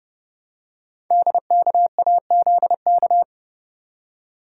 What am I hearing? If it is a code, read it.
Morse code, heard as a keyed tone, about 30 wpm, 700 Hz. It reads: DKAZK
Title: Code Groups - 5 Letters - 30wpm